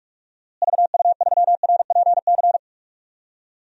Morse code, heard as a keyed tone, about 46 wpm, 700 Hz. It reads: VU3RPC
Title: DXCC-100 Call Signs - 6 Characters - Repeated 2x - 45wpm